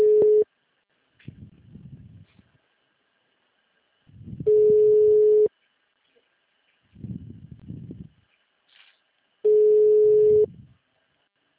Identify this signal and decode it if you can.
Ringback tone (Europe, 425 Hz).